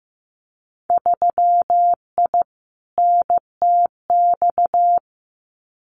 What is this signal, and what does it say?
Morse code, heard as a keyed tone, about 15 wpm, 700 Hz. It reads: 3I NTX
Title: Winter Field Day Exchanges 15wpm